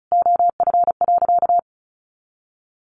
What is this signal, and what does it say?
Morse code, heard as a keyed tone, about 34 wpm, 700 Hz. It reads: OF.
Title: Sentences from Top 300 Words 35wpm